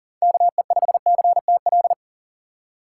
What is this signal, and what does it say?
Morse code, heard as a keyed tone, about 40 wpm, 700 Hz. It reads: KE5CTL
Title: USA Call Signs - 6 Characters - 3x Character Spacing - 40wpm